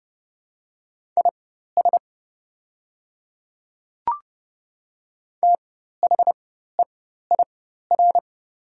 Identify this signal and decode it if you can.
Morse code, heard as a keyed tone, about 30 wpm, spaced as for 14 wpm, 700 Hz. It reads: IS THEIR
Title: 4x Character Spacing - Top 100 Words - 30wpm